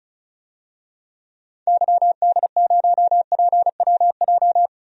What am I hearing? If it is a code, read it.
Morse code, heard as a keyed tone, about 36 wpm, 700 Hz. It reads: YD0PWJ
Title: DXCC-100 Call Signs - 6 Characters - 35wpm